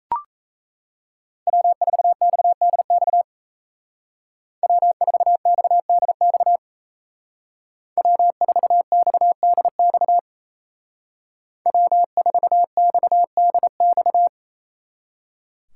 Morse code, heard as a keyed tone, about 33 wpm, 700 Hz. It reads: W4XDX W4XDX W4XDX W4XDX
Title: Speed Racing - USA 1x3 Call Signs 28wpm